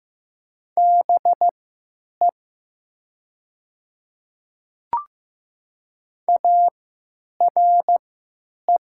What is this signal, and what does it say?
Morse code, heard as a keyed tone, about 15 wpm, spaced as for 9 wpm, 700 Hz. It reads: BE ARE
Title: 3x Character Spacing - Top 100 Words - 15wpm